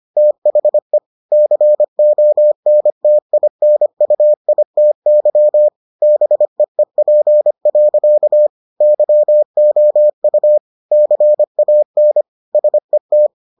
Morse code, heard as a keyed tone, about 25 wpm, 600 Hz. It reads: THE CONTINUITY BEEP. YOU CAN SET